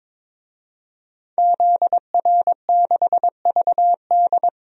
Morse code, heard as a keyed tone, about 22 wpm, 700 Hz. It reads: ZR6VD
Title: DXCC-100 Call Signs - 5 Characters - Repeated 2x - 22wpm